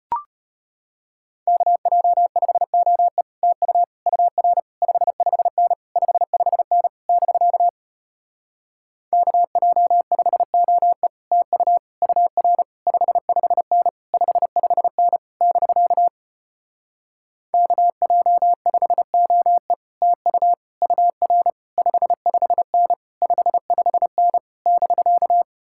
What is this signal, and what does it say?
Morse code, heard as a keyed tone, about 33 wpm, 700 Hz. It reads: KJ5OE TU UR 55N 55N <BK> KJ5OE TU UR 55N 55N <BK> KJ5OE TU UR 55N 55N <BK>